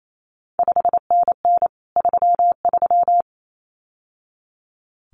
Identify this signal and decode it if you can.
Morse code, heard as a keyed tone, about 28 wpm, 700 Hz. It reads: 5NN 33
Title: CQ WW DX Contest Exchange Encore - 28wpm